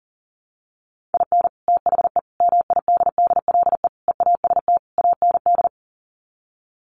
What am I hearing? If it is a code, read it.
Morse code, heard as a keyed tone, about 40 wpm, 700 Hz. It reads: IN THE MIDDLE EAST AND
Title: Sets of 5 Words Encore 40wpm